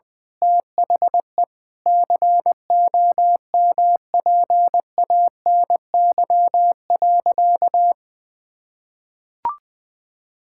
Morse code, heard as a keyed tone, about 21 wpm, 700 Hz. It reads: THE COMPANY.
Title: Sentences with 8-Word Phrases - Spoken Beginning - 20wpm